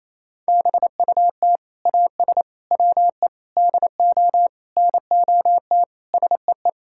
Morse code, heard as a keyed tone, about 28 wpm, 700 Hz. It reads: BUT AS WE DO NOT SEE
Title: Sentences from Top 1600 Words 28wpm